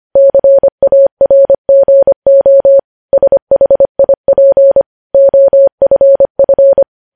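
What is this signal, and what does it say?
Morse code, heard as a keyed tone, about 25 wpm, 550 Hz. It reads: CARGO SHIP OFF
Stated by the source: Morse code news headlines at 25 words per minute.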